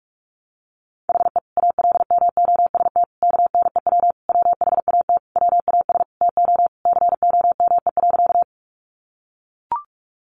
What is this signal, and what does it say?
Morse code, heard as a keyed tone, about 44 wpm, 700 Hz. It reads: HE ALMOST KNEW WHAT WAS TO COME.
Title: Sentences from Top 700 Words 45wpm